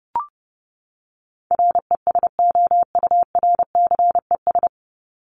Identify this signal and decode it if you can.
Morse code, heard as a keyed tone, about 30 wpm, 700 Hz. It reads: RESOURCES
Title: ES:S Suffix - Top 100 Words with Base Words 30wpm